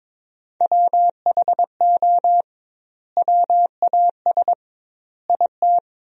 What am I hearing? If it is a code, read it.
Morse code, heard as a keyed tone, about 22 wpm, 700 Hz. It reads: WHO WAS IT
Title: 2x Word-Spacing - Sentences from Top 400 Words - 22wpm